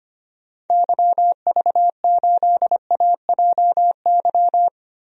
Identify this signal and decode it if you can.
Morse code, heard as a keyed tone, about 25 wpm, 700 Hz. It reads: YV8AJY
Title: DXCC-100 Call Signs - 6 Characters - Repeated 3x - 25wpm